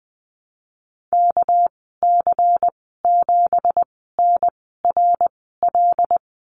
Morse code, heard as a keyed tone, about 19 wpm, spaced as for 14 wpm, 700 Hz. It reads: KC7NRL
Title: 200 Lesson 32g - Letter k - Call Signs 20wpm